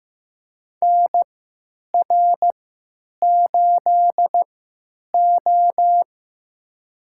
Morse code, heard as a keyed tone, about 15 wpm, spaced as for 8 wpm, 700 Hz. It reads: NR8O
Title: USA Call Signs - 4 Characters - 3x Character Spacing - 15wpm